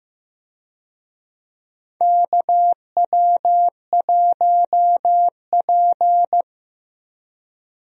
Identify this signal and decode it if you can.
Morse code, heard as a keyed tone, about 15 wpm, 700 Hz. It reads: KW1P
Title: USA Call Signs - 4 Characters - 15wpm